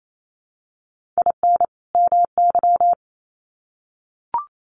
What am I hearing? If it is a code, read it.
Morse code, heard as a keyed tone, about 28 wpm, 700 Hz. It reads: IN MY
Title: Sets of 2 Words 28wpm